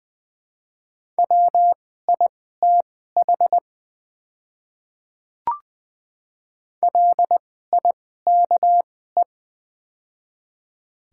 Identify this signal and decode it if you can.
Morse code, heard as a keyed tone, about 21 wpm, spaced as for 15 wpm, 700 Hz. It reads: WITH LIKE
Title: 2x Character Spacing - Top 100 Words - 20wpm